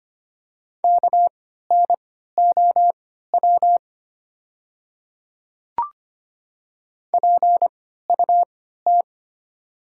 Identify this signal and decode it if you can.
Morse code, heard as a keyed tone, about 25 wpm, spaced as for 14 wpm, 700 Hz. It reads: KNOW PUT
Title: 3x Character Spacing - Top 300 Words - 25wpm